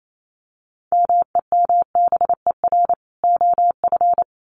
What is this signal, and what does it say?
Morse code, heard as a keyed tone, about 28 wpm, 700 Hz. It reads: MEMBER OF
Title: Sets of 3 Words 28wpm